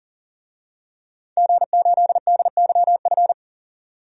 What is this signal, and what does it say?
Morse code, heard as a keyed tone, about 40 wpm, 700 Hz. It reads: G8DYF
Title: DXCC-100 Call Signs - 5 Characters - 40wpm